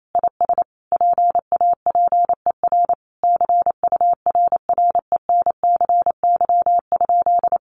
Morse code, heard as a keyed tone, about 28 wpm, 700 Hz. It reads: IS PAPER CURRENCY?
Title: Sentences from Top 1500 Words 28wpm